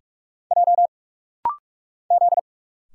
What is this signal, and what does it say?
Morse code, heard as a keyed tone, about 46 wpm, 690 Hz. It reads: J Z
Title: Single Letters - Rapid-Fire 45wpm